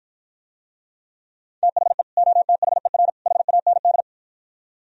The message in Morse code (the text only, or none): THE OTHER HAND